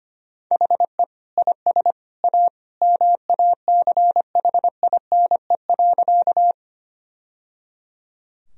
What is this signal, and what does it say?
Morse code, heard as a keyed tone, about 25 wpm, 700 Hz. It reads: HE IS A MACHINE.
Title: Sentences from Top 1000 Words - Review Part 2 - 25wpm